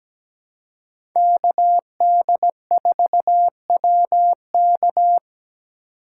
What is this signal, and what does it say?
Morse code, heard as a keyed tone, about 21 wpm, 700 Hz. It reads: KD4WK
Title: Speed Racing - USA 2x2 Call Signs 17wpm